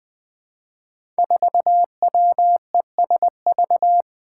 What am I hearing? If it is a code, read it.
Morse code, heard as a keyed tone, about 20 wpm, 700 Hz. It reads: 4WESV
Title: Five Challenging Letter-Number Combinations 20wpm